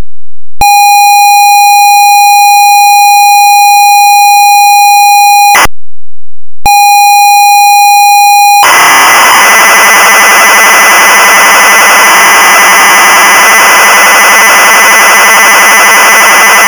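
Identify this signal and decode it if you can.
ZX Spectrum tape loading.
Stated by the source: ZX Spectrum tape loading the game Saboteur 2.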